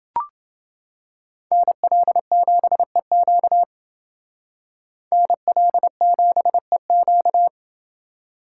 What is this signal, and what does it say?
Morse code, heard as a keyed tone, about 29 wpm, 700 Hz. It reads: NL7EQ NL7EQ